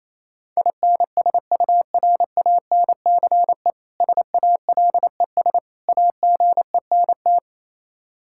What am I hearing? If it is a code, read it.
Morse code, heard as a keyed tone, about 28 wpm, 700 Hz. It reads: INSURANCE SALES AGENT
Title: Practice QSO - Exchange 3 - More Info - 28wpm